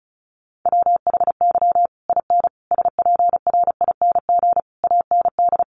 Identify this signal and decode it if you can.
Morse code, heard as a keyed tone, about 34 wpm, 700 Hz. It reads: WHY IN SPRING AND
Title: Sentences from Top 1200 Words 35wpm